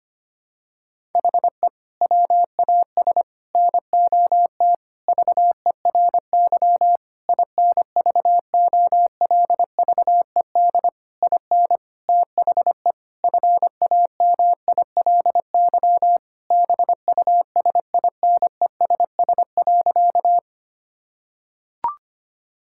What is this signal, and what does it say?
Morse code, heard as a keyed tone, about 25 wpm, 700 Hz. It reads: HE WAS NOT VERY INVOLVED IN THE FAMILY BUSINESS.
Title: Sentences from Top 1000 Words - Review Part 5 - 25wpm